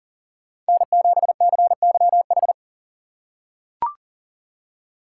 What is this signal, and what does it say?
Morse code, heard as a keyed tone, about 39 wpm, 700 Hz. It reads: N7CYH
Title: USA Call Signs - 5 Characters - 2x Character Spacing - 40wpm